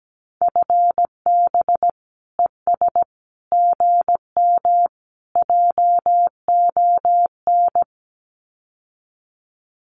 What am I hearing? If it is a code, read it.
Morse code, heard as a keyed tone, about 17 wpm, 700 Hz. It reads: FB ES GM JON
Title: Common QSO Phrases Encore 17wpm